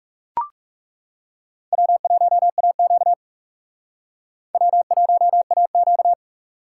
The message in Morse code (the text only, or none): W1AQ W1AQ